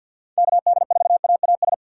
Morse code, heard as a keyed tone, about 48 wpm, 700 Hz. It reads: KD4AAS
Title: USA Call Signs - 4x Character Spacing - 50wpm